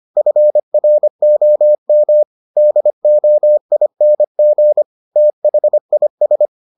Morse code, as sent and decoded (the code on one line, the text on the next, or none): ..-. .-. --- -- / -.. --- .. -. --. / - .... .. ...
FROM DOING THIS